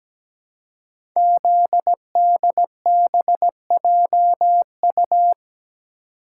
Morse code, heard as a keyed tone, about 17 wpm, 700 Hz. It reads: ZDBJU
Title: Code Groups - 5 Letters - 17wpm